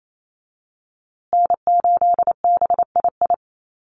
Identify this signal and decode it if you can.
Morse code, heard as a keyed tone, about 28 wpm, 700 Hz. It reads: N8BII
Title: USA Call Signs Repeated 2x 28wpm